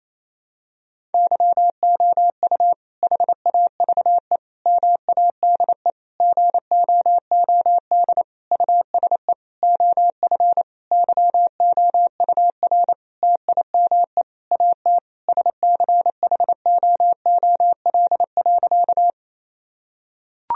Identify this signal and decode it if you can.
Morse code, heard as a keyed tone, about 28 wpm, 700 Hz. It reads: YOU HAVE MADE GOOD USE OF YOUR TIME AT SCHOOL.
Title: Sentences from Top 1000 Words - Review Part 1 - 28wpm